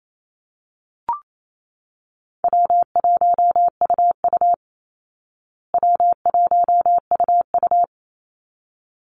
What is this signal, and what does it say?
Morse code, heard as a keyed tone, about 28 wpm, 700 Hz. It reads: W1UU W1UU
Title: Easy Call Signs Repeated 2x - 28wpm